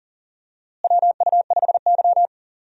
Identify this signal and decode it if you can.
Morse code, heard as a keyed tone, about 38 wpm, 700 Hz. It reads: WU5Y
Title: USA Call Signs - 4 Characters - Repeated 3x - 40wpm